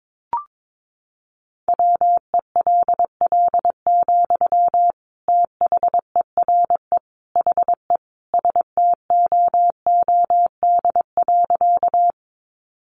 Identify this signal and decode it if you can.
Morse code, heard as a keyed tone, about 22 wpm, 700 Hz. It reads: WELL, THERE HE STOOD.
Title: Sentences from Top 500 Words 22wpm